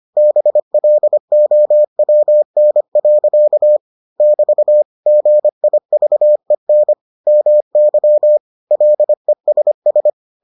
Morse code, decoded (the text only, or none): BLOWN. <BT> GIVEN MY LESS